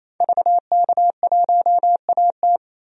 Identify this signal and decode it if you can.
Morse code, heard as a keyed tone, about 28 wpm, 700 Hz. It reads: VK1AT
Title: DXCC-100 Call Signs - 5 Characters - 28wpm